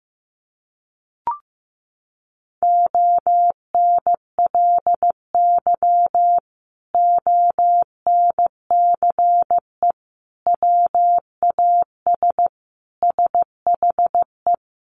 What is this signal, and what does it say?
Morse code, heard as a keyed tone, about 15 wpm, 700 Hz. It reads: ONLY ONCE WAS SHE
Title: Sentences from Top 1100 Words 15wpm